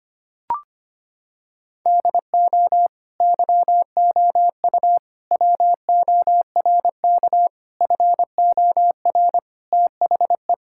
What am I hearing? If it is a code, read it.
Morse code, heard as a keyed tone, about 25 wpm, 700 Hz. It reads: DO YOU WORK FOR THE
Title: Sentences from Top 1000 Words - Review Part 5 - 25wpm